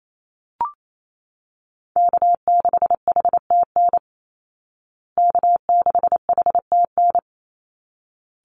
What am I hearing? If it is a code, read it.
Morse code, heard as a keyed tone, about 28 wpm, 700 Hz. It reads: K6HTN K6HTN
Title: Call Signs Repeated 3x - 28wpm